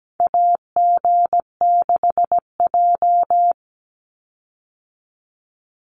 Morse code, heard as a keyed tone, about 17 wpm, 700 Hz. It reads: AG6J
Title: USA Call Signs 17wpm